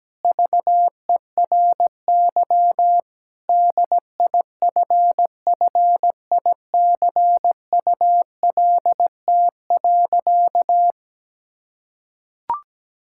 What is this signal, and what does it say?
Morse code, heard as a keyed tone, about 17 wpm, 700 Hz. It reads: VERY DIFFICULT.
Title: Sentences from Top 1000 Words - Review Part 6 - 17wpm